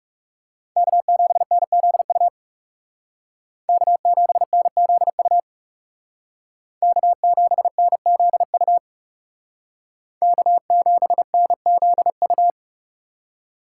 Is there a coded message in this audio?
Yes: Morse code from 0.8 to 12.5 s, heard as a keyed tone, about 38 wpm, 700 Hz. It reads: K7NZU K7NZU K7NZU K7NZU